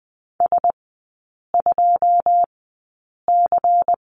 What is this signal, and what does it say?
Morse code, heard as a keyed tone, about 20 wpm, spaced as for 8 wpm, 690 Hz. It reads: S2C
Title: 212 Lesson 34e - Number 8 - Three Characters 20wpm